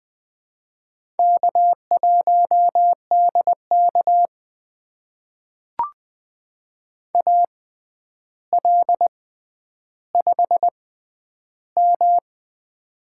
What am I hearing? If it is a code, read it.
Morse code, heard as a keyed tone, about 20 wpm, spaced as for 10 wpm, 700 Hz. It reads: K1DK AL5M
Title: USA Call Signs - 4 Characters - 6x Character Spacing - 20wpm